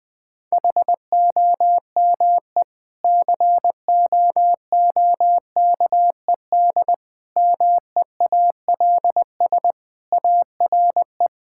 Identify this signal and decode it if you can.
Morse code, heard as a keyed tone, about 20 wpm, 700 Hz. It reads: HOME COOKED MEALS ARE
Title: Difficult Sentences with Hints - 20wpm